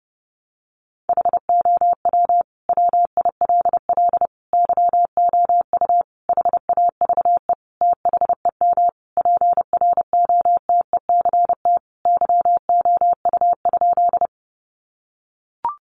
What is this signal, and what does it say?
Morse code, heard as a keyed tone, about 28 wpm, 700 Hz. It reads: HOW WILL YOU HAVE THEM PROTECT YOU?
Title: Sentences from Top 900 Words 30wpm